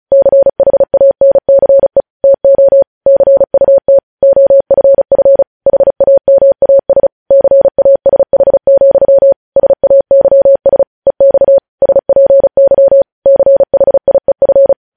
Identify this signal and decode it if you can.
Morse code, heard as a keyed tone, about 36 wpm, 550 Hz. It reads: CHANCE TO CUT OFF HAMAS CASH, SAYS EX SPY CHIEF